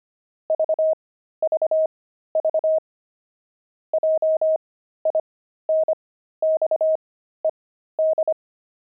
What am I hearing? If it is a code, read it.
Morse code, heard as a keyed tone, about 25 wpm, spaced as for 13 wpm, 630 Hz. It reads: VVV JINXED
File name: pangram-fs-13-cs-25-jinxed